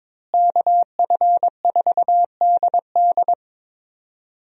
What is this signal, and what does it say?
Morse code, heard as a keyed tone, about 22 wpm, 700 Hz. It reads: KF4DD
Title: USA Call Signs - 5 Characters - Repeated 3x - 22wpm